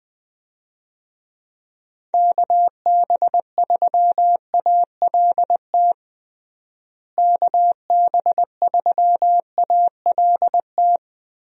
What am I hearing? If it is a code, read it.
Morse code, heard as a keyed tone, about 21 wpm, 700 Hz. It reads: KB3ALT KB3ALT
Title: USA Call Signs - 6 Characters - Repeated 3x - 20wpm